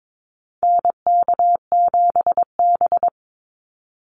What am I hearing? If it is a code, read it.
Morse code, heard as a keyed tone, about 22 wpm, 700 Hz. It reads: NK7B
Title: Call Signs Repeated 2x - 22wpm